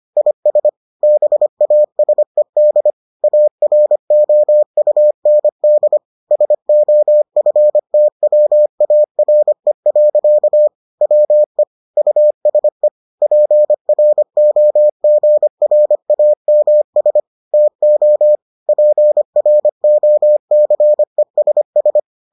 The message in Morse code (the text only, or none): IS BASED AROUND SOFTWARE. WE USE PROGRAMS TO PROCESS